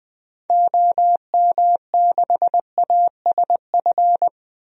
Morse code, heard as a keyed tone, about 21 wpm, 700 Hz. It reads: OM6ASF
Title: DXCC-100 Call Signs - 6 Characters - Repeated 3x - 20wpm